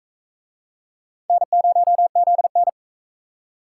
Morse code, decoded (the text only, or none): N0ZN